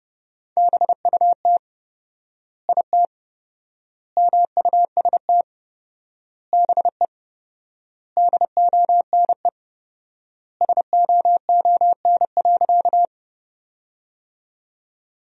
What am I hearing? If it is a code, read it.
Morse code, heard as a keyed tone, about 32 wpm, 700 Hz. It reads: BUT IT MUST BE DONE SOON.
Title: 4x Word-Spacing - Sentences from Top 500 Words - 30wpm